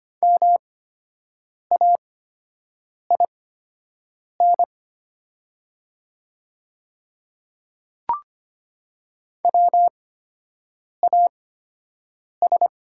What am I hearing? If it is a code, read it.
Morse code, heard as a keyed tone, about 25 wpm, spaced as for 7 wpm, 700 Hz. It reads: MAIN WAS
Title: 8x Character Spacing - Top 300 Words - 25wpm